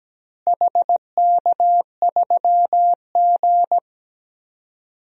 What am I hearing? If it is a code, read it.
Morse code, heard as a keyed tone, about 17 wpm, 700 Hz. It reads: HK3G
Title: DXCC-100 Call Signs - 4 Characters - Repeated 3x - 17wpm